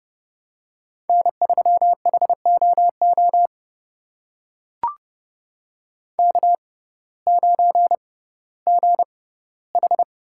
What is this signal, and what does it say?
Morse code, heard as a keyed tone, about 31 wpm, spaced as for 17 wpm, 700 Hz. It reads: N3HOO K9GH
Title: USA Call Signs - 5 Characters - 6x Character Spacing - 30wpm